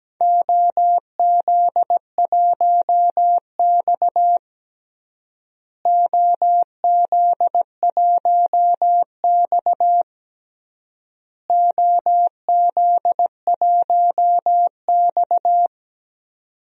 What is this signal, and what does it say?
Morse code, heard as a keyed tone, about 17 wpm, 700 Hz. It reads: OZ1X OZ1X OZ1X
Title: DXCC-100 Call Signs - 4 Characters - Repeated 3x - 17wpm